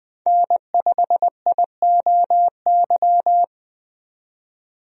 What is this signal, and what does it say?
Morse code, heard as a keyed tone, about 20 wpm, 700 Hz. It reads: N5IOY
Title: USA Call Signs - 5 Characters - 20wpm